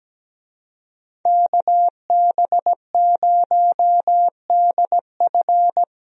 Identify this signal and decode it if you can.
Morse code, heard as a keyed tone, about 17 wpm, 700 Hz. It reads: KB0DF
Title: USA Call Signs - 5 Characters - Repeated 2x - 17wpm